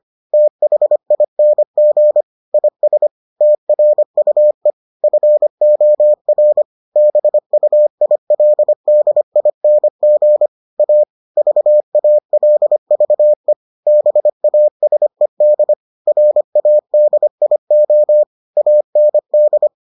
Morse code, heard as a keyed tone, about 25 wpm, 600 Hz. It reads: THING IS TRUE FOR BUILDING A VALVE BASED RADIO AND